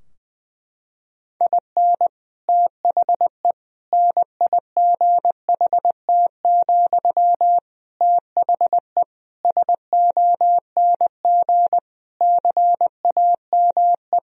Morse code, heard as a keyed tone, about 20 wpm, 700 Hz. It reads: IN THE NIGHT, THE SONG CAME
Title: Sentences from Top 1000 Words - Review Part 2 - 20wpm